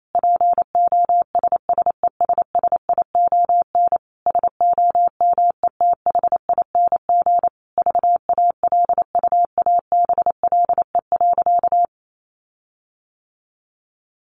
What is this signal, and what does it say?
Morse code, heard as a keyed tone, about 28 wpm, 700 Hz. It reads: POSSESSION SOMETHING VALUABLE.
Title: Sentences from Top 1400 Words 28wpm